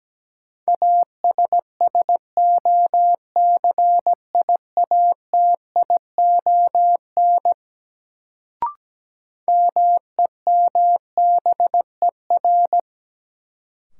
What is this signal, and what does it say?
Morse code, heard as a keyed tone, about 17 wpm, 700 Hz. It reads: ASSOCIATION MEMBER
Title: Sentences from Top 1100 Words - New Word Drill - 17wpm